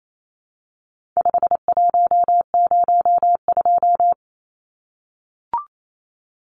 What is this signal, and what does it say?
Morse code, heard as a keyed tone, about 28 wpm, 700 Hz. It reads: 5102
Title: Numbers - 4-Digit 28wpm